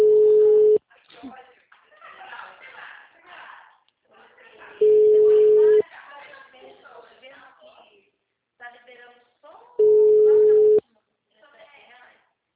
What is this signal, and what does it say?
Ringback tone (Europe, 425 Hz), with speech.